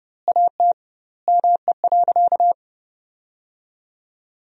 Morse code, heard as a keyed tone, about 32 wpm, 700 Hz. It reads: AT ME.
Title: 2x Word-Spacing - Sentences from Top 200 Words - 30wpm